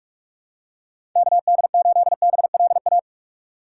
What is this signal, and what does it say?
Morse code, heard as a keyed tone, about 45 wpm, 700 Hz. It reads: KD8BLA
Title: USA Call Signs - 6 Characters - 2x Character Spacing - 45wpm